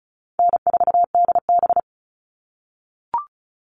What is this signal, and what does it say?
Morse code, heard as a keyed tone, about 34 wpm, 700 Hz. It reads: N4DB
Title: All Cut Numbers - 4-Digit 35wpm